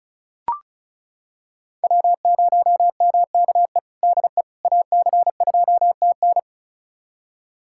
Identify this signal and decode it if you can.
Morse code, heard as a keyed tone, about 34 wpm, 700 Hz. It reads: W0MKE DE AC2TN
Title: USA Callsign DE Callsign - 5 Characters - 35wpm